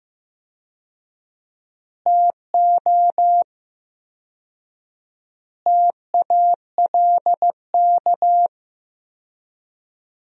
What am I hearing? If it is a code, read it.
Morse code, heard as a keyed tone, about 15 wpm, 700 Hz. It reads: TO TALK
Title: 4x Word-Spacing - Sentences from Top 400 Words - 15wpm